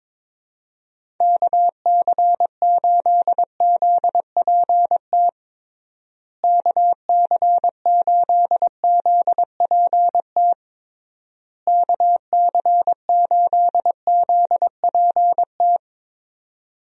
Morse code, heard as a keyed tone, about 22 wpm, 700 Hz. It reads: KC8ZPT KC8ZPT KC8ZPT
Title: USA Call Signs - 6 Characters - Repeated 3x - 22wpm